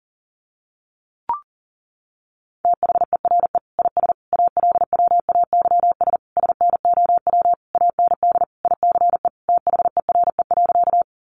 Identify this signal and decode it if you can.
Morse code, heard as a keyed tone, about 40 wpm, 700 Hz. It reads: THERE IS ALWAYS SNOW AND ICE THERE.
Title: Sentences from Top 1000 Words 40wpm